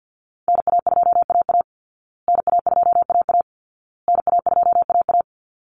Morse code, heard as a keyed tone, about 50 wpm, 700 Hz. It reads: NA2AA NA2AA NA2AA
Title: Call Signs Repeated 2x - 50wpm